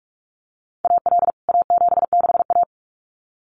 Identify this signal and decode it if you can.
Morse code, heard as a keyed tone, about 44 wpm, 700 Hz. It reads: AL A76A
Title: CWT Exchange 45wpm